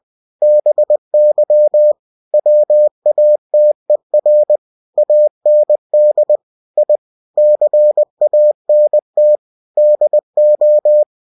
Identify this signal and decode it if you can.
Morse code, heard as a keyed tone, about 20 wpm, 600 Hz. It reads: BY WATER AND I CANT DO
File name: cw-books / winnie-the-pooh / winnie-20wpm-page-0091